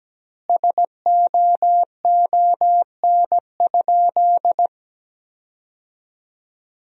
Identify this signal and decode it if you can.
Morse code, heard as a keyed tone, about 17 wpm, 700 Hz. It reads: SOON?
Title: 3x Word-Spacing - Sentences from Top 400 Words - 17wpm